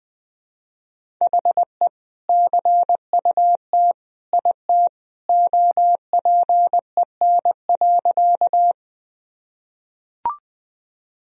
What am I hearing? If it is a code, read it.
Morse code, heard as a keyed tone, about 21 wpm, 700 Hz. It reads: HE CUT IT OPEN.
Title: Sentences from Top 1000 Words - Review Part 1 - 20wpm